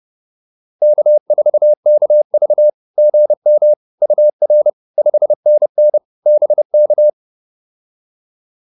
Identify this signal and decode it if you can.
Morse code, heard as a keyed tone, about 30 wpm, 600 Hz. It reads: K4KV GM UR 5NN BK